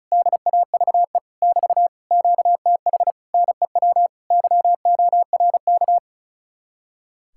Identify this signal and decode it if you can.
Morse code, heard as a keyed tone, about 43 wpm, 700 Hz. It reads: DAVE <BT> QTH NEW YORK
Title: Speed-Racing Common QSO Phrases Encore - 35wpm